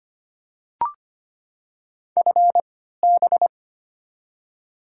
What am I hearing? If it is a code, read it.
Morse code, heard as a keyed tone, about 25 wpm, spaced as for 14 wpm, 700 Hz. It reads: FB
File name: Top 20 QSO Elements - 3x Character Spacing - 25wpm